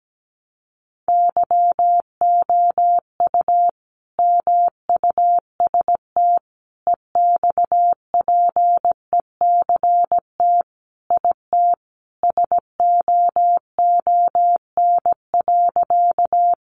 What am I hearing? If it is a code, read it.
Morse code, heard as a keyed tone, about 17 wpm, 700 Hz. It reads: YOU MUST EXPECT IT SOON.
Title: Sentences from Top 1100 Words 17wpm